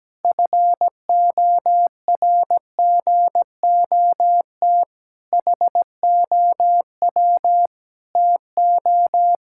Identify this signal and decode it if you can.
Morse code, heard as a keyed tone, about 17 wpm, 700 Hz. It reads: FORGOT HOW TO
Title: Pangram Challenge v2 - 17wpm